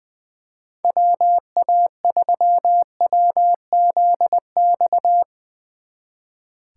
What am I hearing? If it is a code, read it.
Morse code, heard as a keyed tone, about 22 wpm, 700 Hz. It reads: WA3WZX
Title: Speed Racing - USA 2x3 Call Signs 20wpm